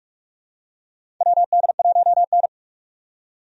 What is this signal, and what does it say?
Morse code, heard as a keyed tone, about 41 wpm, 700 Hz. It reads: WD1N
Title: USA Call Signs - 4 Characters - 45wpm